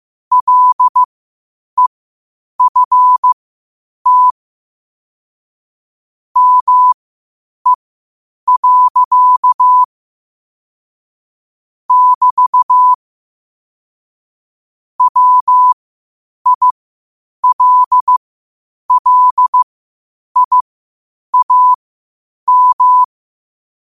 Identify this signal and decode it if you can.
Morse code, heard as a keyed tone, about 15 wpm, spaced as for 8 wpm, 1000 Hz. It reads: LEFT ME. <BT> WILLIAM
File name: qotdfiles / 8wpm